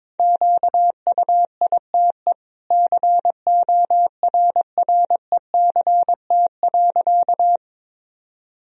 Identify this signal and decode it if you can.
Morse code, heard as a keyed tone, about 22 wpm, 700 Hz. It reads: QUITE CORRECT.
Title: Sentences from Top 1000 Words - Review Part 2 - 22wpm